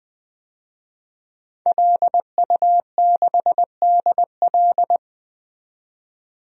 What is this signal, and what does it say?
Morse code, heard as a keyed tone, about 20 wpm, 700 Hz. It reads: LU6DL
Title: DXCC-100 Call Signs - 5 Characters - 20wpm